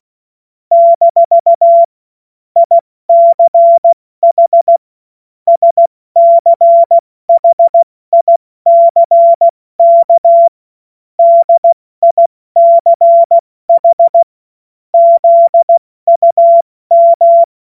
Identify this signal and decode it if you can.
Morse code, heard as a keyed tone, about 16 wpm, spaced as for 14 wpm, 680 Hz. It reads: - ICH SCHICK DICH ZUM
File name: Morsekurs / Witze / Witz 10. August 2021, 70 F80